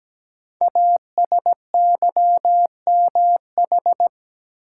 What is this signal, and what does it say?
Morse code, heard as a keyed tone, about 17 wpm, 700 Hz. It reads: ASYMH
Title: Code Groups - 5 Letters - 17wpm